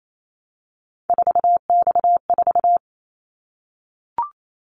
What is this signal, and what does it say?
Morse code, heard as a keyed tone, about 28 wpm, 700 Hz. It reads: 4X4
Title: Callsign Prefixes for Top 50 DXCC - 28wpm